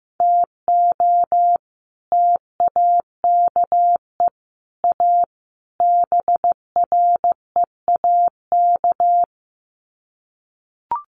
Morse code, heard as a keyed tone, about 15 wpm, 700 Hz. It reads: TO TAKE A BREAK
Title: Sets of 4 Words 15wpm